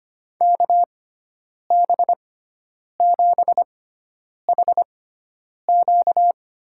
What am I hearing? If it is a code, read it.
Morse code, heard as a keyed tone, about 25 wpm, spaced as for 9 wpm, 700 Hz. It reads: KB7HQ
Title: USA Call Signs - 6x Character Spacing - 25wpm